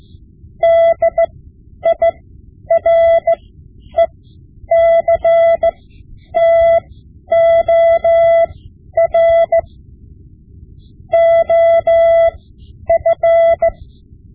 Morse code, heard as a keyed tone, about 15 wpm, spaced as for 10 wpm, 670 Hz. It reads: DIRECTOR OF
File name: Morse Lessons / Morse Lessons / Tape-5B